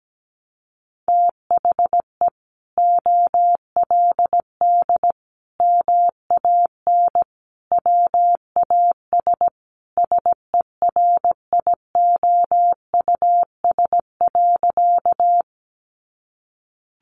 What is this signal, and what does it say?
Morse code, heard as a keyed tone, about 17 wpm, 700 Hz. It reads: THE OLD MAN WAS SERIOUS.
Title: Sentences from Top 1100 Words 17wpm